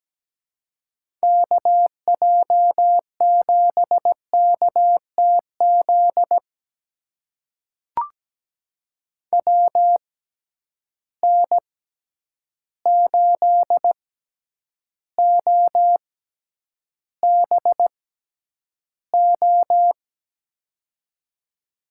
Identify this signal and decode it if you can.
Morse code, heard as a keyed tone, about 17 wpm, spaced as for 9 wpm, 700 Hz. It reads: KJ7KTZ WN8OBO
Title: USA Call Signs - 6 Characters - 6x Character Spacing - 17wpm